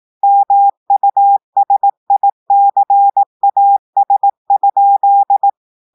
Morse code, heard as a keyed tone, about 18 wpm, 800 Hz. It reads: MUSICAS?
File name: audios / portugais / portugais texte en clair 2 vitesse 18 mots minutes